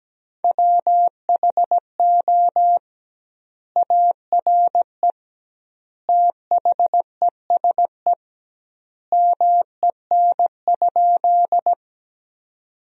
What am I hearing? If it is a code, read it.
Morse code, heard as a keyed tone, about 17 wpm, 700 Hz. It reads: WHO ARE THESE MEN?
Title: 2x Word-Spacing - Sentences from Top 200 Words - 17wpm